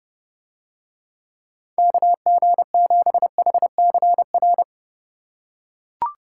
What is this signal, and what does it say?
Morse code, heard as a keyed tone, about 27 wpm, 700 Hz. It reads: KG7HCR